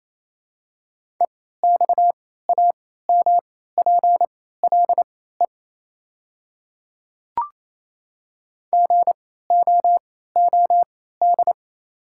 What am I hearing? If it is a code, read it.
Morse code, heard as a keyed tone, about 28 wpm, spaced as for 16 wpm, 700 Hz. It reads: EXAMPLE GOOD
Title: 3x Character Spacing - Top 300 Words - 28wpm